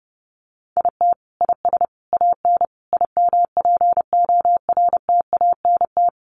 Morse code, heard as a keyed tone, about 30 wpm, 700 Hz. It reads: IT IS AN IMPORTANT
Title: Sentences from Top 1300 Words 30wpm